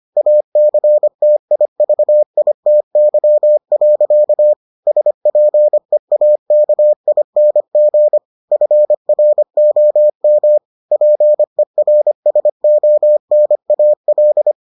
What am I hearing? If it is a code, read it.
Morse code, heard as a keyed tone, about 25 wpm, 600 Hz. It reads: ACTIVITY. SPEAKING FROM PERSONAL